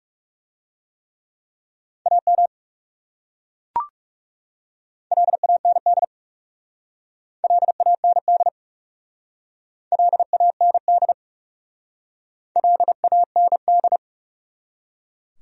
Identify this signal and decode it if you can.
Morse code, heard as a keyed tone, about 39 wpm, 700 Hz. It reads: AM LAND LAND LAND LAND